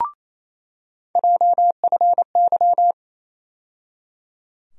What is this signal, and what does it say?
Morse code, heard as a keyed tone, about 28 wpm, 700 Hz. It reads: JFY
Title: USA Call Sign Postfixes - 3 Letters 28wpm